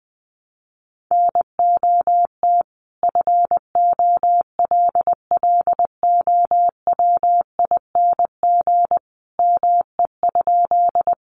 Morse code, heard as a keyed tone, about 20 wpm, 700 Hz. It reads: NOT FOLLOWING ME?
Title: Sentences from Top 700 Words 20wpm